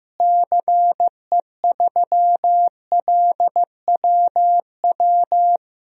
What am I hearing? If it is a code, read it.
Morse code, heard as a keyed tone, about 15 wpm, 700 Hz. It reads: CE3LWW
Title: DXCC-100 Call Signs - 6 Characters - Repeated 3x - 15wpm